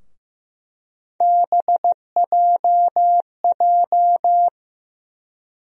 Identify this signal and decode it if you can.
Morse code, heard as a keyed tone, about 15 wpm, 700 Hz. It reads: BJJ
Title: USA Call Sign Postfixes - 3 Letters 15wpm